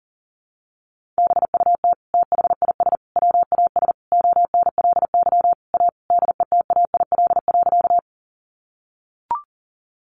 Morse code, heard as a keyed tone, about 40 wpm, 700 Hz. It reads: BUT THIS WAS ONLY A DETAIL.
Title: Sentences from Top 1500 Words 40wpm